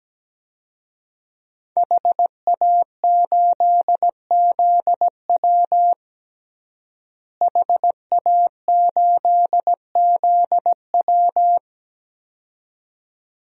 Morse code, heard as a keyed tone, about 17 wpm, 700 Hz. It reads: HA8ZW HA8ZW
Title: DXCC-100 Call Signs - 5 Characters - Repeated 2x - 17wpm